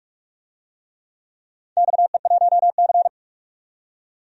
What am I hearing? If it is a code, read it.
Morse code, heard as a keyed tone, about 44 wpm, 700 Hz. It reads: XE1C
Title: DXCC-100 Call Signs - 4 Characters - 45wpm